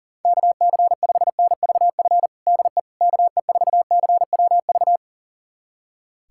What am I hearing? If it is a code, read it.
Morse code, heard as a keyed tone, about 40 wpm, 700 Hz. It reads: KC5NVF DE KE4CWV
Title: USA Callsign DE Callsign 40wpm